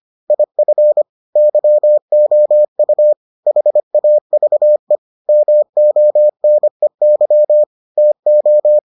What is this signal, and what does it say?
Morse code, heard as a keyed tone, about 25 wpm, 600 Hz. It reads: IF YOU HAVE MONEY TO